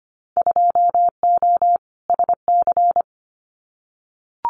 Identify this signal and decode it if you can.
Morse code, heard as a keyed tone, about 25 wpm, 700 Hz. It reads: 2O SC
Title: Winter Field Day Exchanges Encore 25wpm